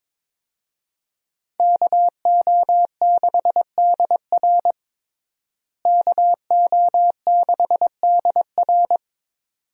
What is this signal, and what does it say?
Morse code, heard as a keyed tone, about 22 wpm, 700 Hz. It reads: KO6DR KO6DR
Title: USA Call Signs - 5 Characters - Repeated 2x - 22wpm